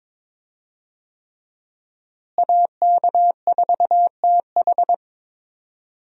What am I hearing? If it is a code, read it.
Morse code, heard as a keyed tone, about 21 wpm, 700 Hz. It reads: AK4TH
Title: USA Call Signs - 5 Characters - Repeated 3x - 22wpm